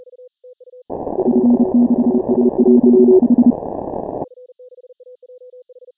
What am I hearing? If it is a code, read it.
Morse code, heard as a keyed tone, about 30 wpm, 240 Hz. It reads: F6AJH
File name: final / W1UJ-168-140-MorseRunner